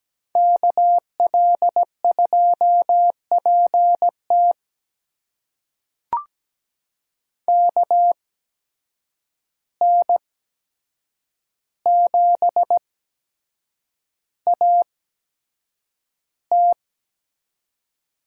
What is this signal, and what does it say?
Morse code, heard as a keyed tone, about 17 wpm, spaced as for 7 wpm, 700 Hz. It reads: KL2PT KN7AT
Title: USA Call Signs - 5 Characters - 8x Character Spacing - 17wpm